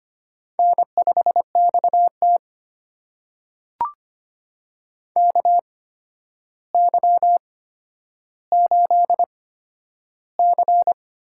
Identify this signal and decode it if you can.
Morse code, heard as a keyed tone, about 25 wpm, spaced as for 11 wpm, 700 Hz. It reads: N5XT KY8C
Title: USA Call Signs - 4 Characters - 8x Character Spacing - 25wpm